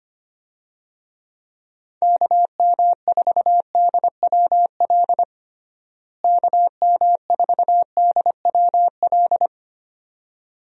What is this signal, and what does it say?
Morse code, heard as a keyed tone, about 25 wpm, 700 Hz. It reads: KM4DWL KM4DWL
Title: USA Call Signs - 6 Characters - Repeated 2x - 25wpm